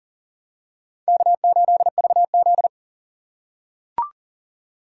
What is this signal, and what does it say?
Morse code, heard as a keyed tone, about 40 wpm, 700 Hz. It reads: K8VZ
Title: USA Call Signs - 4 Characters - 3x Character Spacing - 40wpm